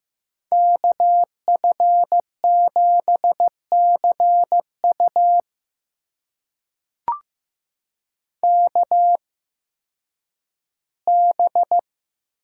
Morse code, heard as a keyed tone, about 15 wpm, spaced as for 10 wpm, 700 Hz. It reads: KF7CU KB
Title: USA Call Signs - 5 Characters - 8x Character Spacing - 15wpm